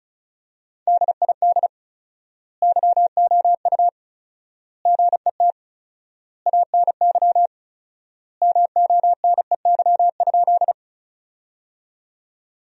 Morse code, heard as a keyed tone, about 36 wpm, 700 Hz. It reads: DID YOU GET ANY MONEY?
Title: 4x Word-Spacing - Sentences from Top 400 Words - 35wpm